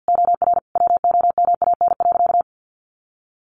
Morse code, heard as a keyed tone, about 50 wpm, 700 Hz. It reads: OR WOMAN.
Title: Sentences from Top 1400 Words 50wpm